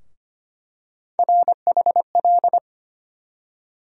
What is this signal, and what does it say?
Morse code, heard as a keyed tone, about 25 wpm, 700 Hz. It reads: RHL
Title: USA Call Sign Postfixes - 3 Letters 25wpm